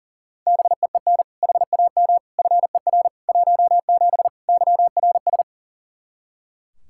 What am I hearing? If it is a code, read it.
Morse code, heard as a keyed tone, about 40 wpm, 700 Hz. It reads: BEEN HAM FER 17 YRS
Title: K1USN SST QSO Practice 40wpm